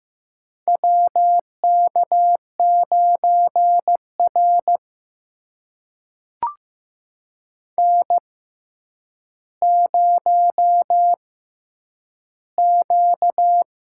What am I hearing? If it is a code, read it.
Morse code, heard as a keyed tone, about 16 wpm, spaced as for 9 wpm, 700 Hz. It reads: WK9R N0Q
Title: USA Call Signs - 6x Character Spacing - 15wpm